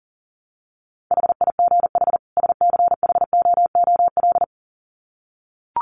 Morse code, heard as a keyed tone, about 40 wpm, 700 Hz. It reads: HIGH SCHOOL
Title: Sets of 2 Words 40wpm